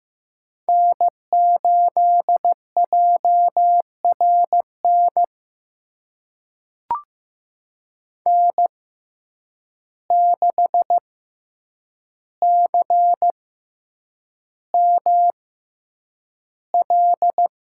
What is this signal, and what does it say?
Morse code, heard as a keyed tone, about 15 wpm, spaced as for 8 wpm, 700 Hz. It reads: N8JRN N6CML
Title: USA Call Signs - 5 Characters - 6x Character Spacing - 15wpm